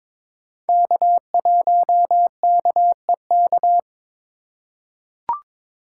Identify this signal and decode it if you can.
Morse code, heard as a keyed tone, about 22 wpm, 700 Hz. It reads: K1KEK
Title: USA Call Signs - 5 Characters - 2x Character Spacing - 22wpm